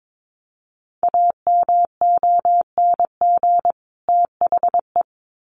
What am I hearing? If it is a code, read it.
Morse code, heard as a keyed tone, about 22 wpm, 700 Hz. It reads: AMONG THE
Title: Sets of 2 Words 22wpm